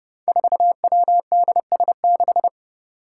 Morse code, heard as a keyed tone, about 30 wpm, 700 Hz. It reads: 4WDS6
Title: Five Challenging Letter-Number Combinations 30wpm